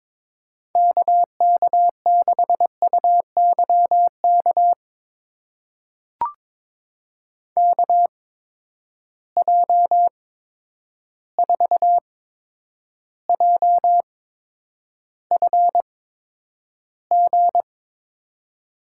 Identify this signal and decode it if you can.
Morse code, heard as a keyed tone, about 22 wpm, spaced as for 9 wpm, 700 Hz. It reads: KK6UYK KJ4JFG
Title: USA Call Signs - 6 Characters - 8x Character Spacing - 22wpm